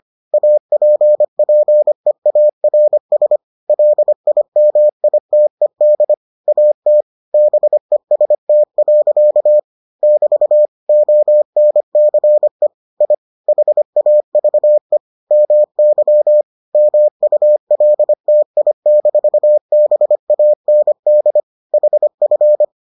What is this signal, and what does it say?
Morse code, heard as a keyed tone, about 25 wpm, 600 Hz. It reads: APPEARS LIMITED AT BEST. <BT> ONCE I HAVE MY MULTI-BAND HF